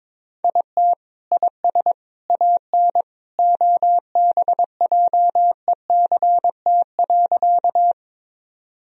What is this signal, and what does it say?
Morse code, heard as a keyed tone, about 22 wpm, 700 Hz. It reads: IT IS AN OBJECT.
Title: Sentences from Top 1000 Words - Review Part 3 - 22wpm